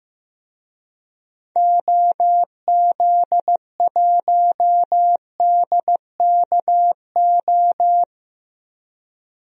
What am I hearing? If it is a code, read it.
Morse code, heard as a keyed tone, about 15 wpm, 700 Hz. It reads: OZ1DKO
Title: DXCC-100 Call Signs - 6 Characters - 15wpm